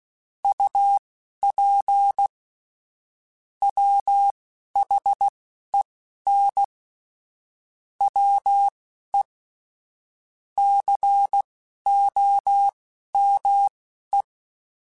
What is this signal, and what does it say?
Morse code, heard as a keyed tone, about 16 wpm, spaced as for 12 wpm, 800 Hz. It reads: UP WHEN WE COME
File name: QSOs / QSOs-mp3s / WD0BGP 2@10